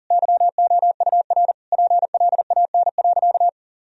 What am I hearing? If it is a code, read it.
Morse code, heard as a keyed tone, about 40 wpm, 700 Hz. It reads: YOUR PLAN.
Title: Sentences from Top 1000 Words - Review Part 2 - 40wpm